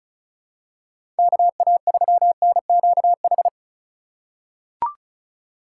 Morse code, heard as a keyed tone, about 36 wpm, 700 Hz. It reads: KA3NQH
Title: USA Call Signs - 2x Character Spacing - 35wpm